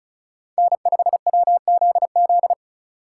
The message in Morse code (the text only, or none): N5WZZ